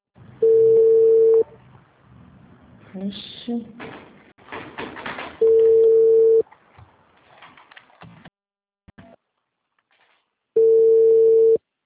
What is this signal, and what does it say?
Ringback tone (China, 450 Hz), with speech.